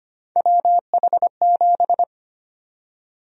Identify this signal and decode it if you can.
Morse code, heard as a keyed tone, about 25 wpm, 700 Hz. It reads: WH7
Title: USA Call Sign Prefixes - 25wpm